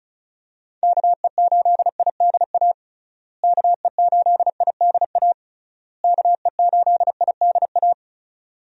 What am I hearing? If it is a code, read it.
Morse code, heard as a keyed tone, about 36 wpm, 700 Hz. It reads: KE8IDA KE8IDA KE8IDA
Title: USA Call Signs - 6 Characters - Repeated 3x - 35wpm